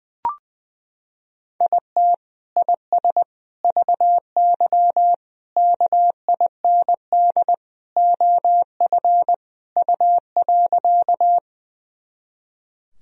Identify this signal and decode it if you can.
Morse code, heard as a keyed tone, about 20 wpm, 700 Hz. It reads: IT IS VY KIND OF U.
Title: Sentences from Top 200 Words and Abbreviations 20wpm